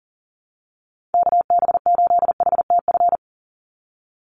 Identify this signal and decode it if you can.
Morse code, heard as a keyed tone, about 40 wpm, 700 Hz. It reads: KB8HTF
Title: USA Call Signs 40wpm